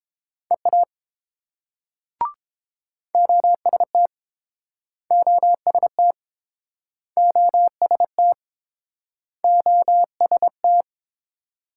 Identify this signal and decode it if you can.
Morse code, heard as a keyed tone, about 32 wpm, 700 Hz. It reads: EA OST OST OST OST